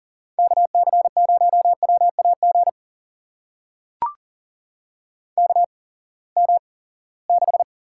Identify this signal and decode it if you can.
Morse code, heard as a keyed tone, about 40 wpm, spaced as for 23 wpm, 700 Hz. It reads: KC0WAG KM6
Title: USA Call Signs - 6 Characters - 8x Character Spacing - 40wpm